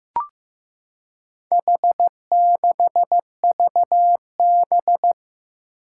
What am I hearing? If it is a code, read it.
Morse code, heard as a keyed tone, about 15 wpm, 700 Hz. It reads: H6VB
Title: Four Challenging Letter-Number Combinations 15wpm